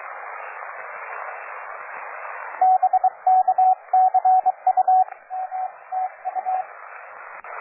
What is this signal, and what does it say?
Morse code, heard as a keyed tone, about 23 wpm, 750 Hz, with radio noise. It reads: BK CU M TU
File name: cw forum / OJ0B sur 1823 le 21-09-2008 17h00 test GP XXL